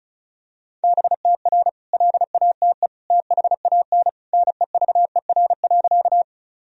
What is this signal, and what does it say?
Morse code, heard as a keyed tone, about 36 wpm, 700 Hz. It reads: BTR LATE THAN NEVER.
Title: Sentences from Top 400 Words and Abbreviations 35wpm